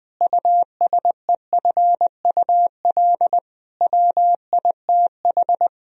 Morse code, heard as a keyed tone, about 20 wpm, 700 Hz. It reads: USEFUL WITH
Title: Sentences from Top 1000 Words - Review Part 4 - 20wpm